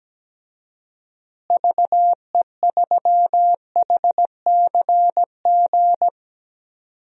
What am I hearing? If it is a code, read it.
Morse code, heard as a keyed tone, about 17 wpm, 700 Hz. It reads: VE3HCG
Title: DXCC-100 Call Signs - 6 Characters - Repeated 3x - 17wpm